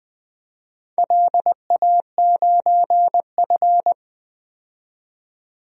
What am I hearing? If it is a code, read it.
Morse code, heard as a keyed tone, about 20 wpm, 700 Hz. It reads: LA9F
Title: DXCC-100 Call Signs - 4 Characters - 20wpm